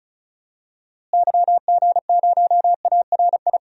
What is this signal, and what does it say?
Morse code, heard as a keyed tone, about 36 wpm, 700 Hz. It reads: YG0ARI
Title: DXCC-100 Call Signs - 6 Characters - Repeated 3x - 35wpm